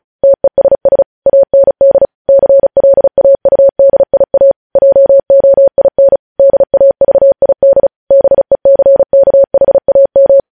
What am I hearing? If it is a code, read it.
Morse code, heard as a keyed tone, about 36 wpm, 550 Hz. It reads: TESS AND CLAUDIA JOIN DAVID BECKHAM
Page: news headlines in morse code at 35 wpm